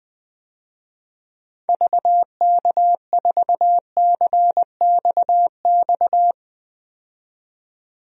Morse code, heard as a keyed tone, about 21 wpm, 700 Hz. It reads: VK4CXX
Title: DXCC-100 Call Signs - 6 Characters - 20wpm